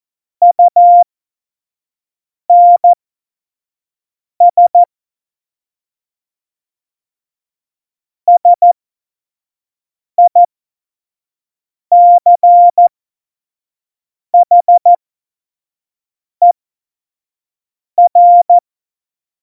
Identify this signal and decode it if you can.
Morse code, heard as a keyed tone, about 14 wpm, spaced as for 5 wpm, 700 Hz. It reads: UNS SICHER
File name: Morsekurs / Witze / Witz 29. Dezember 2020, 25 F70